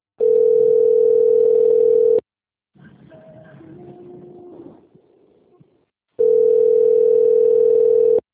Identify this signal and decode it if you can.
Ringback tone (North America), 440 + 480 Hz.
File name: rg-1600-unknown- - - .16906